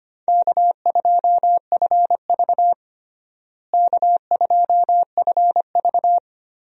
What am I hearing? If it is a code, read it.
Morse code, heard as a keyed tone, about 25 wpm, 700 Hz. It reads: K2FV K2FV
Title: USA Call Signs - 4 Characters - Repeated 3x - 25wpm